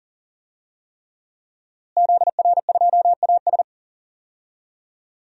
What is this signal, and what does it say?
Morse code, heard as a keyed tone, about 40 wpm, 700 Hz. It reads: ZR2AS
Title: DXCC-100 Call Signs - 5 Characters - Repeated 3x - 40wpm